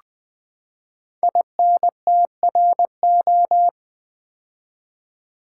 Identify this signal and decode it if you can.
Morse code, heard as a keyed tone, about 20 wpm, 700 Hz. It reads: INTRO
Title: Build A Word - 5 Letters At A Time - 12-Letter Words - 20wpm